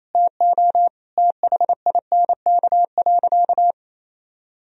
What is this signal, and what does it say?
Morse code, heard as a keyed tone, about 28 wpm, 700 Hz. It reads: TO THINK.
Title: Sentences from Top 1000 Words - Review Part 1 - 28wpm